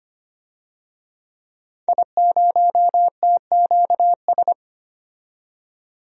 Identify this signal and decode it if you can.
Morse code, heard as a keyed tone, about 25 wpm, 700 Hz. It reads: I0TQS
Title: DXCC-100 Call Signs - 5 Characters - 25wpm